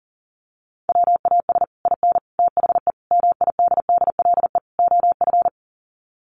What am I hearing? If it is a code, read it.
Morse code, heard as a keyed tone, about 40 wpm, 700 Hz. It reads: WAS IN THE MIDDLE OF
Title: Sets of 5 Words Encore 40wpm